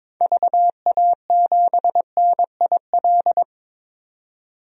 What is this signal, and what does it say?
Morse code, heard as a keyed tone, about 21 wpm, 700 Hz. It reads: VA7NIL
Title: DXCC-100 Call Signs - 6 Characters - 22wpm